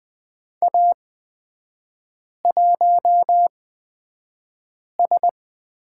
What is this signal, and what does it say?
Morse code, heard as a keyed tone, about 20 wpm, spaced as for 5 wpm, 690 Hz. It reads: A1S
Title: 229 Lesson 37e - Letter x - Three Characters 20wpm